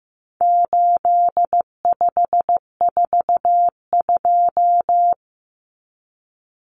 Morse code, heard as a keyed tone, about 15 wpm, 700 Hz. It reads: 8542
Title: Common Cut Numbers - 4-Digit 15wpm